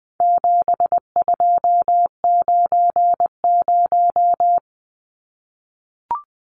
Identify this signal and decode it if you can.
Morse code, heard as a keyed tone, about 20 wpm, 700 Hz. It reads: 7290
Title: Numbers - 4-Digit 20wpm